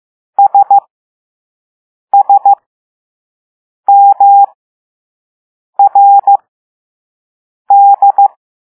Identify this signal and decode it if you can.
Morse code, heard as a keyed tone, about 15 wpm, spaced as for 5 wpm, 800 Hz. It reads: SSMRD